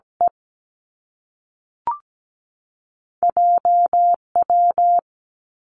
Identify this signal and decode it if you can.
Morse code, heard as a keyed tone, about 17 wpm, 700 Hz. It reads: E JW
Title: Two Challenging Letter-Number Combinations 17wpm